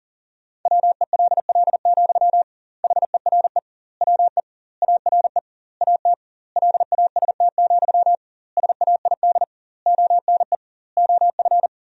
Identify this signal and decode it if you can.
Morse code, heard as a keyed tone, about 37 wpm, 700 Hz. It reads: WELL, HERE WE ARE AT LAST, SAID ONE OF
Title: 2x Word-Spacing - Sentences from Top 300 Words - 40wpm